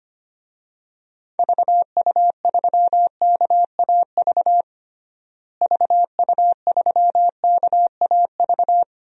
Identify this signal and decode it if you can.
Morse code, heard as a keyed tone, about 25 wpm, 700 Hz. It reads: VU3KAV VU3KAV
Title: DXCC-100 Call Signs - 6 Characters - Repeated 3x - 25wpm